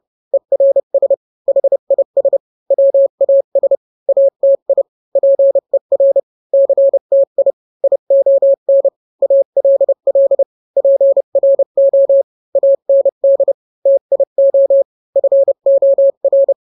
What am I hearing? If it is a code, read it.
Morse code, heard as a keyed tone, about 30 wpm, 550 Hz. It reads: ERS HIS WAS ATI PER CTI ION ALL PRO AND TIO FOR